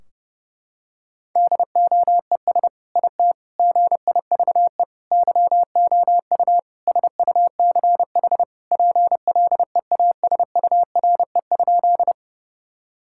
Morse code, heard as a keyed tone, about 32 wpm, 700 Hz. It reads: DOES IT GIVE YOU SUCH PLEASURE?
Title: Sentences from Top 1600 Words 30wpm